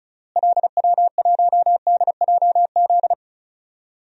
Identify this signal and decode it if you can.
Morse code, heard as a keyed tone, about 36 wpm, 700 Hz. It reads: LW1DJZ
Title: DXCC-100 Call Signs - 6 Characters - Repeated 2x - 35wpm